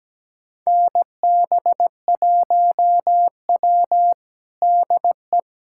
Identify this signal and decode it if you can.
Morse code, heard as a keyed tone, about 17 wpm, 700 Hz. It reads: NB1W DE
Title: USA Callsign DE Callsign - 4 Characters - 17wpm